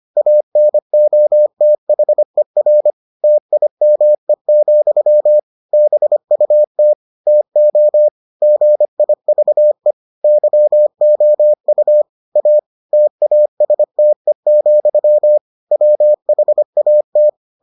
Morse code, heard as a keyed tone, about 25 wpm, 600 Hz. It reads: ANOTHER TIME, BUT TO GIVE YOU A TASTE, WHAT